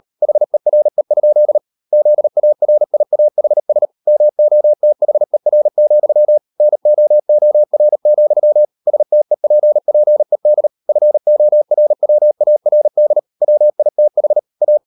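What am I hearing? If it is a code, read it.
Morse code, heard as a keyed tone, about 38 wpm, 600 Hz. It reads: HERE? ZARIAHS MOTHER, NOOR, STEPPED FORWARD WITH A
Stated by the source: Morse code practice at 38 WPM, sending Ken Follett's novel Never in English, from CW med Gnister.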